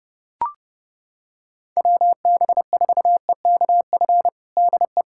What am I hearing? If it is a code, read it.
Morse code, heard as a keyed tone, about 31 wpm, 700 Hz. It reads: WB4EKF DE
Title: USA Callsign DE Callsign - 6 Characters - 30wpm